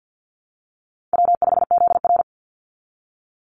Morse code, heard as a keyed tone, about 48 wpm, 700 Hz. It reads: W5ZR
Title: Easy Call Signs Repeated 2x - 50wpm